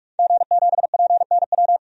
Morse code, heard as a keyed tone, about 46 wpm, 700 Hz. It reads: G7PNW
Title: DXCC-100 Call Signs - 5 Characters - 45wpm